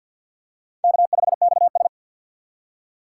Morse code, heard as a keyed tone, about 53 wpm, 700 Hz. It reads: K5XS
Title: USA Call Signs - 4 Characters - 3x Character Spacing - 50wpm